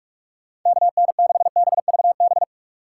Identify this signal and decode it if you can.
Morse code, heard as a keyed tone, about 43 wpm, 700 Hz. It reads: KN6BVB
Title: USA Call Signs - 6 Characters - 6x Character Spacing - 45wpm